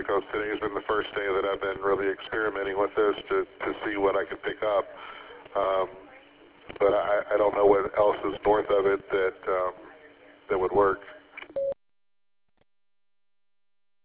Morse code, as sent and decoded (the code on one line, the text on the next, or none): none